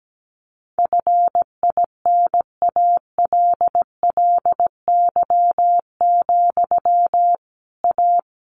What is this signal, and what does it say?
Morse code, heard as a keyed tone, about 17 wpm, 700 Hz. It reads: FINALLY, A
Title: Sentences from Top 1200 Words 17wpm